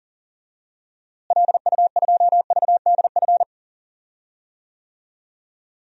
Morse code, heard as a keyed tone, about 40 wpm, 700 Hz. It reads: LU2VDF
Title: DXCC-100 Call Signs - 6 Characters - Repeated 3x - 40wpm